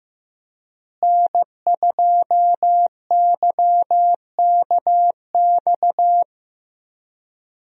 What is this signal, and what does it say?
Morse code, heard as a keyed tone, about 15 wpm, 700 Hz. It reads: N2YKX
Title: USA Call Signs - 5 Characters - Repeated 2x - 15wpm